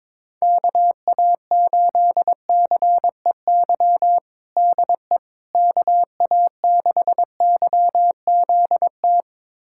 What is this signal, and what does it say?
Morse code, heard as a keyed tone, about 22 wpm, 700 Hz. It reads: KA8CEY DE KA6YZT
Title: USA Callsign DE Callsign - 6 Characters - 22wpm